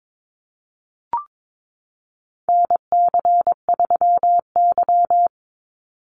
Morse code, heard as a keyed tone, about 22 wpm, 700 Hz. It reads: NC3Y
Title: Call Signs Repeated 3x - 22wpm